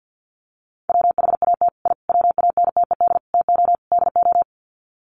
Morse code, heard as a keyed tone, about 50 wpm, 700 Hz. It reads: WHAT I WANTED TO DO
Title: Sets of 5 Words Encore2 50wpm